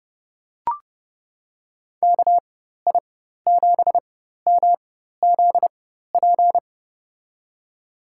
Morse code, heard as a keyed tone, about 31 wpm, spaced as for 14 wpm, 700 Hz. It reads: KI7MZP
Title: USA Call Signs - 4x Character Spacing - 30wpm